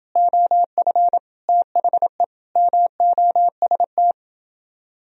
Morse code, heard as a keyed tone, about 29 wpm, 700 Hz. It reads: OF THE MOST